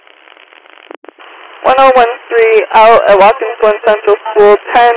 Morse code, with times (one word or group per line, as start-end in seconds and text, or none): none